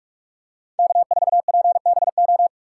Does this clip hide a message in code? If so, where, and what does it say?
Yes: Morse code from 0.8 to 2.5 s, heard as a keyed tone, about 44 wpm, 700 Hz. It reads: K4PBO